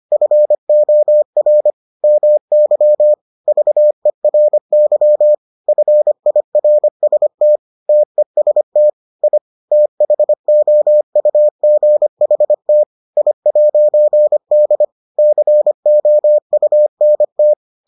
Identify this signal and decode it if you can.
Morse code, heard as a keyed tone, about 25 wpm, 600 Hz. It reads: FOR MY VERY FIRST TEST I THOUGHT I'D COUNT